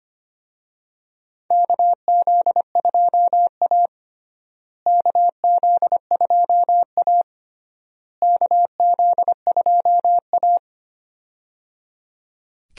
Morse code, heard as a keyed tone, about 25 wpm, 700 Hz. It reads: KZ2A KZ2A KZ2A
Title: USA Call Signs - 4 Characters - Repeated 3x - 25wpm